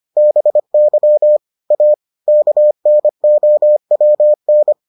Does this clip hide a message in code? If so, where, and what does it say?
Yes: Morse code from 0.2 to 4.7 s, heard as a keyed tone, about 25 wpm, 600 Hz. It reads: BY A KNOWN